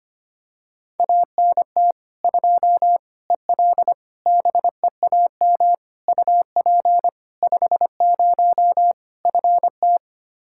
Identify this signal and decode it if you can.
Morse code, heard as a keyed tone, about 25 wpm, 700 Hz. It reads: ANT 2 EL BEAM UP 50 FT
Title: Practice QSO - Exchange 2 - Info - 25wpm